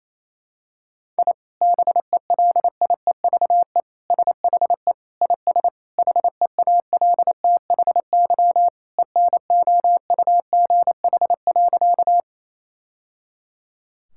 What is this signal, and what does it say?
Morse code, heard as a keyed tone, about 28 wpm, 700 Hz. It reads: I BELIEVE SHE IS HEALTHY ENOUGH.
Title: Sentences from Top 1000 Words - Review Part 5 - 28wpm